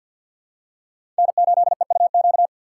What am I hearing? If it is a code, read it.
Morse code, heard as a keyed tone, about 47 wpm, 700 Hz. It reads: N8EUQ
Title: USA Call Signs - 5 Characters - 4x Character Spacing - 50wpm